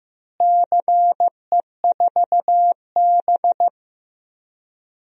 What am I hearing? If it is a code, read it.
Morse code, heard as a keyed tone, about 15 wpm, 700 Hz. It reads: CE4B
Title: DXCC-100 Call Signs - 4 Characters - Repeated 2x - 15wpm